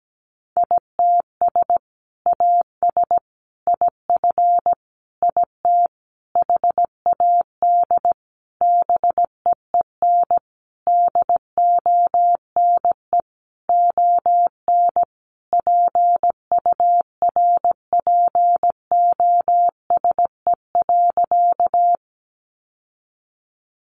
Morse code, heard as a keyed tone, about 17 wpm, 700 Hz. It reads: ITS AS IF IT HAD BEEN DONE ON PURPOSE.
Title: Sentences from Top 800 Words 17wpm